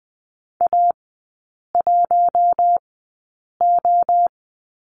Morse code, heard as a keyed tone, about 20 wpm, spaced as for 8 wpm, 700 Hz. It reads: A1O
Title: 087 Lesson 15e - Number 5 - Three Characters 20wpm